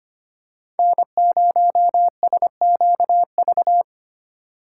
Morse code, heard as a keyed tone, about 25 wpm, 700 Hz. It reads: N0SQV
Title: USA Call Signs - 5 Characters - 3x Character Spacing - 25wpm